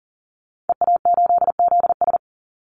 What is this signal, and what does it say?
Morse code, heard as a keyed tone, about 40 wpm, 700 Hz. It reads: EA8ZS
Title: Call Signs Repeated 3x - 40wpm